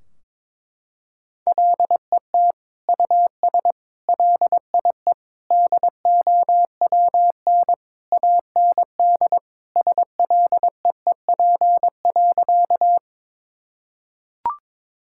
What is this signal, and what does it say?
Morse code, heard as a keyed tone, about 21 wpm, 700 Hz. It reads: LET US LIE DOWN AND SLEEP.
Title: Sentences from Top 1600 Words 22wpm